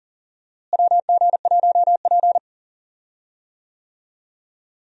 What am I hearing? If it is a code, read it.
Morse code, heard as a keyed tone, about 40 wpm, 700 Hz. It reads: WG1P
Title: USA Call Signs - 4 Characters - Repeated 2x - 40wpm